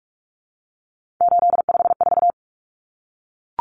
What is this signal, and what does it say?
Morse code, heard as a keyed tone, about 44 wpm, 700 Hz. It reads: 854
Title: Common Cut Numbers - 3-Digit 45wpm